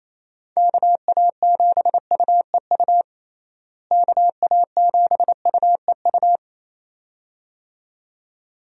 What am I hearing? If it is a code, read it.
Morse code, heard as a keyed tone, about 28 wpm, 700 Hz. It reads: KA7UEU KA7UEU
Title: USA Call Signs - 6 Characters - Repeated 2x - 28wpm